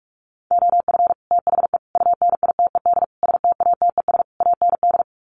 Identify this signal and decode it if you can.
Morse code, heard as a keyed tone, about 44 wpm, 700 Hz. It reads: OF THE UNITED STATES AND
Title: Sets of 5 Words 45wpm